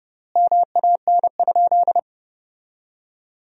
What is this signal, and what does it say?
Morse code, heard as a keyed tone, about 30 wpm, 700 Hz. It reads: MAN?
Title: 4x Word-Spacing - Sentences from Top 200 Words - 30wpm